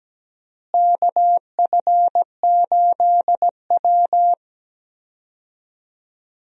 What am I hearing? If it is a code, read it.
Morse code, heard as a keyed tone, about 17 wpm, 700 Hz. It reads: KF8W
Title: USA Call Signs - 4 Characters - Repeated 3x - 17wpm